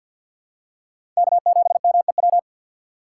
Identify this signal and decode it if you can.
Morse code, heard as a keyed tone, about 52 wpm, 700 Hz. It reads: K7MEW
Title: USA Call Signs - 5 Characters - 6x Character Spacing - 50wpm